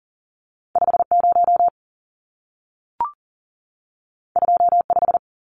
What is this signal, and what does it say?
Morse code, heard as a keyed tone, about 40 wpm, 700 Hz. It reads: 50 25
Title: Numbers - 2-Digit 40wpm